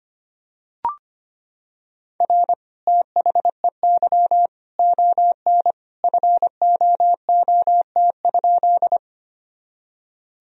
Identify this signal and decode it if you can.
Morse code, heard as a keyed tone, about 25 wpm, 700 Hz. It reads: R THEY ON FOOT?
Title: Sentences from Top 500 Words and Abbreviations 25wpm